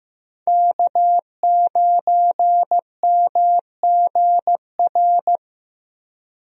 Morse code, heard as a keyed tone, about 15 wpm, 700 Hz. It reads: K9MGR
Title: USA Call Signs - 5 Characters - Repeated 3x - 15wpm